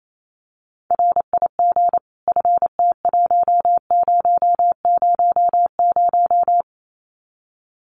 Morse code, heard as a keyed tone, about 28 wpm, 700 Hz. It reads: RIG FT1000
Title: Common QSO Phrases 28wpm